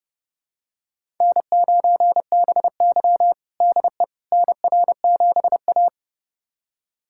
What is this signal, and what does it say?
Morse code, heard as a keyed tone, about 30 wpm, 700 Hz. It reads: N9BY DE NR7A
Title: USA Callsign DE Callsign - 4 Characters - 30wpm